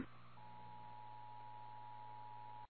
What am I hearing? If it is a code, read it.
Emergency Alert System (SAME) tones (the 853 and 960 Hz attention tone), with noise. It is heard from 0.5 to 2.7 s.
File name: eas / KWBE / KWBE